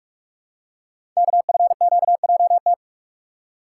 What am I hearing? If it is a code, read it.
Morse code, heard as a keyed tone, about 44 wpm, 700 Hz. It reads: KFQJT
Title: Code Groups - 5 Letters - 45wpm